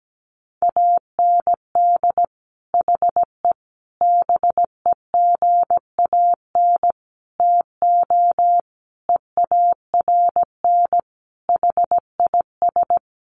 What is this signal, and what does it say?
Morse code, heard as a keyed tone, about 17 wpm, 700 Hz. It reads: AND HE BEGAN TO EARN HIS
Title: Sentences from Top 1500 Words 17wpm